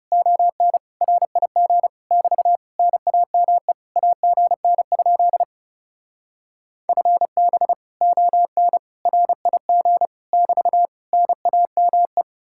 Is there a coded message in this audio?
Yes: Morse code from 0.1 to 12.2 s, heard as a keyed tone, about 39 wpm, 700 Hz. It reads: ON RIG <BT> NAME AGN? FB ON RIG <BT> NAME